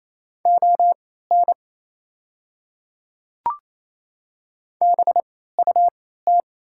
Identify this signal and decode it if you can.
Morse code, heard as a keyed tone, about 28 wpm, spaced as for 16 wpm, 690 Hz. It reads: ON BUT
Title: 3x Character Spacing - Top 100 Words - 28wpm